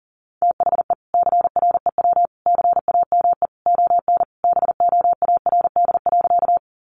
Morse code, heard as a keyed tone, about 40 wpm, 700 Hz. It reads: THE CREW CAME ON BOARD.
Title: Sentences from Top 1500 Words 40wpm